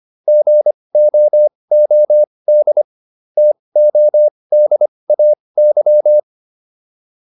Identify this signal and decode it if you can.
Morse code, heard as a keyed tone, about 25 wpm, spaced as for 20 wpm, 600 Hz. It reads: GOOD TODAY